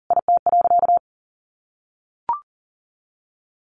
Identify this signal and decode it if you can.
Morse code, heard as a keyed tone, about 40 wpm, 700 Hz. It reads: IT.
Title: Sentences from Top 500 Words 40wpm